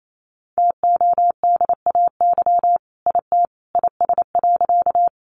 Morse code, heard as a keyed tone, about 28 wpm, 700 Hz. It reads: TODAY IT IS.
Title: Sentences from Top 600 Words 28wpm